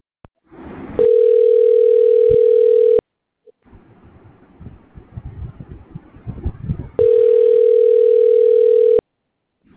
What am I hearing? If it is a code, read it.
Ringback tone (North America), 440 + 480 Hz.